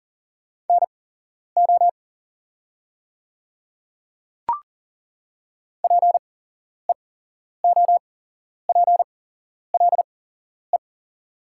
Morse code, heard as a keyed tone, about 40 wpm, spaced as for 11 wpm, 690 Hz. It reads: NO PEOPLE
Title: 8x Character Spacing - Top 100 Words - 40wpm